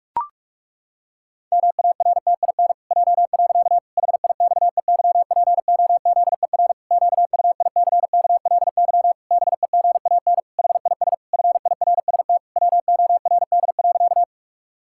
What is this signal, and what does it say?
Morse code, heard as a keyed tone, about 46 wpm, 700 Hz. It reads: MARTIN J. HIXEYPOZER QUICKLY BEGAN HIS FIRST WORD.
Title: Pangram Challenge v2 - 45wpm